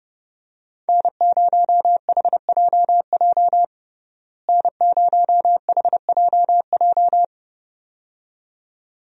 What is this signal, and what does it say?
Morse code, heard as a keyed tone, about 30 wpm, 700 Hz. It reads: N0HJJ N0HJJ
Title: USA Call Signs - 5 Characters - Repeated 2x - 30wpm